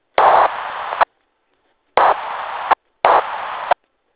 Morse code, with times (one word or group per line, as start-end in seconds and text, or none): none